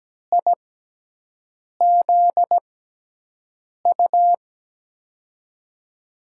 Morse code, heard as a keyed tone, about 17 wpm, spaced as for 6 wpm, 700 Hz. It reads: IZU
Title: USA Call Signs - 6x Character Spacing - 17wpm